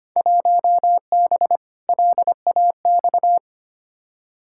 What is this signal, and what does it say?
Morse code, heard as a keyed tone, about 25 wpm, 700 Hz. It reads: 1B LAX
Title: Field Day Exchanges - 25wpm